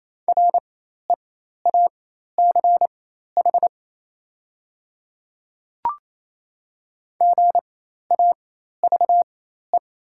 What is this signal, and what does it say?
Morse code, heard as a keyed tone, about 28 wpm, spaced as for 13 wpm, 700 Hz. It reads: REACH GAVE
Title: 4x Character Spacing - Top 500 Words - 28wpm